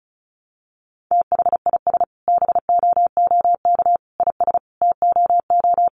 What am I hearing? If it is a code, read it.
Morse code, heard as a keyed tone, about 34 wpm, 700 Hz. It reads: THIS BOOK IS TOO
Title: Additional Easy Phrases 35wpm